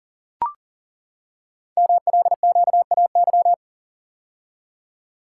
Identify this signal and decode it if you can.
Morse code, heard as a keyed tone, about 40 wpm, 700 Hz. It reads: MLQAY
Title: Code Groups - 5 Letters - 40wpm